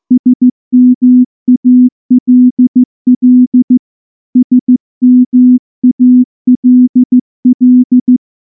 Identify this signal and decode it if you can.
Morse code, heard as a keyed tone, about 14 wpm, 250 Hz. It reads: SMALL SMALL
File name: morse / morse-training-15WPM